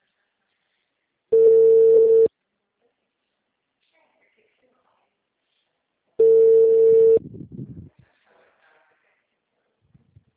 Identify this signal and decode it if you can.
Ringback tone (China, 450 Hz), with speech.